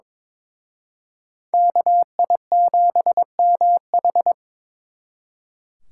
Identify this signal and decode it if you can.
Morse code, heard as a keyed tone, about 26 wpm, 700 Hz. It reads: KI7MH
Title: Speed Racing - USA 2x2 Call Signs 22wpm